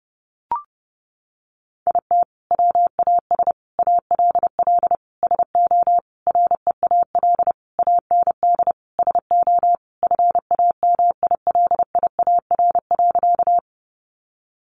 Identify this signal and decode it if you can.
Morse code, heard as a keyed tone, about 30 wpm, 700 Hz. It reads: IT WAS ALL SO REAL AND SO FAMILIAR.
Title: Sentences from Top 1000 Words 30wpm